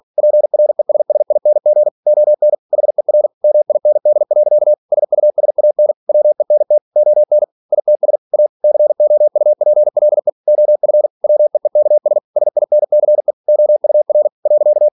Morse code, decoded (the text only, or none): PRESSING ON HER MIND. SUSAN WENT ON ITS A COUPLE OF WEEKS SINCE OUR .